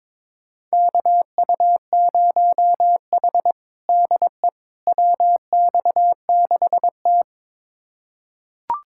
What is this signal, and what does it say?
Morse code, heard as a keyed tone, about 22 wpm, 700 Hz. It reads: KU0H DE WX6T
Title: USA Callsign DE Callsign - 4 Characters - 22wpm